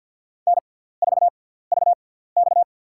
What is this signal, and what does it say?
Morse code, heard as a keyed tone, about 52 wpm, spaced as for 17 wpm, 700 Hz. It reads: N4VX
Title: USA Call Signs - 4 Characters - 6x Character Spacing - 50wpm